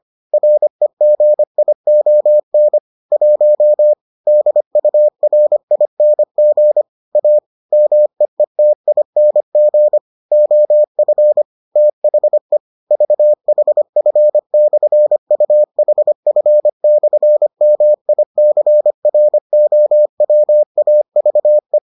Morse code, heard as a keyed tone, about 25 wpm, 600 Hz. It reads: REGION 1 DURING A MEETING OF THE VHF/UHF/MICROWAVE